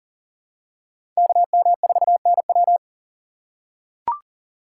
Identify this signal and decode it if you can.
Morse code, heard as a keyed tone, about 40 wpm, 700 Hz. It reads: KM4NW
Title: USA Call Signs - 5 Characters - 4x Character Spacing - 40wpm